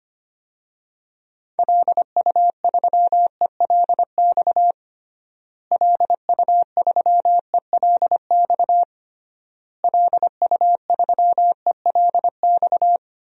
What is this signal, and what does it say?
Morse code, heard as a keyed tone, about 25 wpm, 700 Hz. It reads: LU3ELX LU3ELX LU3ELX
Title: DXCC-100 Call Signs - 6 Characters - Repeated 3x - 25wpm